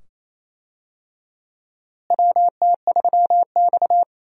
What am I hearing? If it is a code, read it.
Morse code, heard as a keyed tone, about 28 wpm, 700 Hz. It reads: WT3X
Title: USA Call Signs - 4 Characters - Repeated 2x - 28wpm